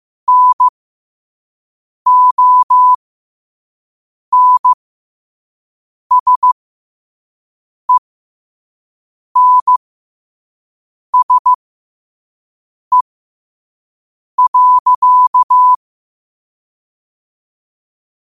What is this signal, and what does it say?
Morse code, heard as a keyed tone, about 15 wpm, spaced as for 5 wpm, 1000 Hz. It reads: NONSENSE.
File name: qotdfiles / 5wpm